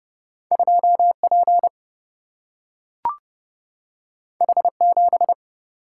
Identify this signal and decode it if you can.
Morse code, heard as a keyed tone, about 30 wpm, 700 Hz. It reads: 2 P H 7
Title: Single-Letter Single-Number 30wpm